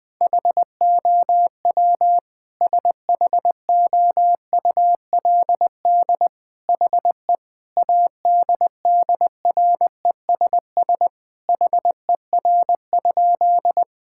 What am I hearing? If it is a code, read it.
Morse code, heard as a keyed tone, about 20 wpm, 700 Hz. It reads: HOW SHOULD HE ADDRESS HER?
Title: Sentences from Top 1000 Words - Review Part 6 - 20wpm